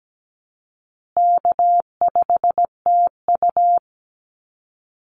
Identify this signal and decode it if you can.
Morse code, heard as a keyed tone, about 17 wpm, 700 Hz. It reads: K5TU
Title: Call Signs Repeated 2x - 17wpm